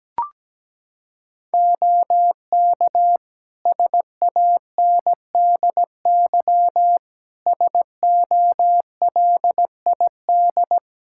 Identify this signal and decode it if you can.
Morse code, heard as a keyed tone, about 17 wpm, 700 Hz. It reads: OK SANDY SOLID
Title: Practice QSO - Exchange 3 - More Info - 17wpm